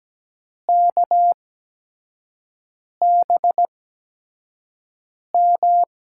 Morse code, heard as a keyed tone, about 17 wpm, spaced as for 5 wpm, 700 Hz. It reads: KBM
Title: USA Call Signs - 6 Characters - 8x Character Spacing - 17wpm